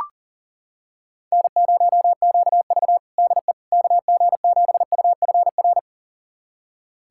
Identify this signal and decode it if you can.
Morse code, heard as a keyed tone, about 40 wpm, 700 Hz. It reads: N0QV DE KG7UFR
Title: USA Callsign DE Callsign 40wpm